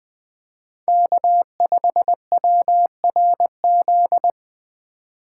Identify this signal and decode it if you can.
Morse code, heard as a keyed tone, about 20 wpm, 700 Hz. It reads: K5WRZ
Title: USA Call Signs - 5 Characters - 3x Character Spacing - 20wpm